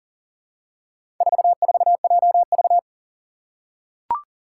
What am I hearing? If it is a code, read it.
Morse code, heard as a keyed tone, about 40 wpm, 700 Hz. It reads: 44JV
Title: Four Challenging Letter-Number Combinations 40wpm